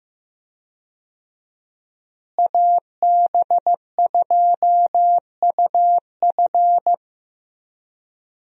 Morse code, heard as a keyed tone, about 15 wpm, 700 Hz. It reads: AB2UF
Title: USA Call Signs - 5 Characters - Repeated 2x - 15wpm